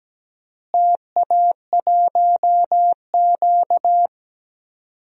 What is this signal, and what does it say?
Morse code, heard as a keyed tone, about 17 wpm, 700 Hz. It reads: TA1Q
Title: DXCC-100 Call Signs - 4 Characters - Repeated 2x - 17wpm